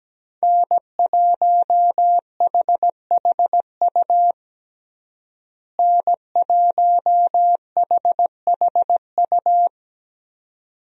Morse code, heard as a keyed tone, about 17 wpm, 700 Hz. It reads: N1HHU N1HHU
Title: USA Call Signs - 5 Characters - Repeated 2x - 17wpm